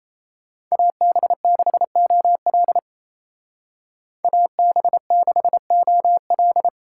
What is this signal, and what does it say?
Morse code, heard as a keyed tone, about 37 wpm, 700 Hz. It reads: AB6OL AB6OL